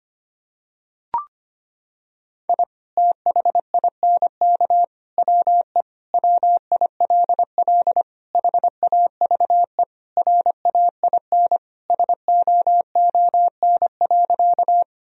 Morse code, heard as a keyed tone, about 25 wpm, 700 Hz. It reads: I THINK WE WILL HAVE RAIN SOON.
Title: Sentences from Top 1000 Words - Review Part 1 - 25wpm